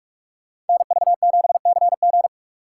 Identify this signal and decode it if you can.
Morse code, heard as a keyed tone, about 42 wpm, 700 Hz. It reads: NV7CG